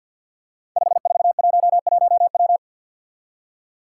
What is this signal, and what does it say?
Morse code, heard as a keyed tone, about 52 wpm, 700 Hz. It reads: 5411W
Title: Five Challenging Letter-Number Combinations 50wpm